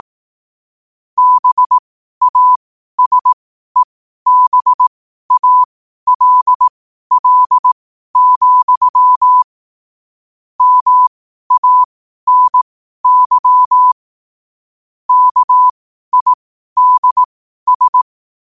Morse code, heard as a keyed tone, about 18 wpm, spaced as for 12 wpm, 1000 Hz. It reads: BASEBALL, MANY KIDS
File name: qotdfiles / 12wpm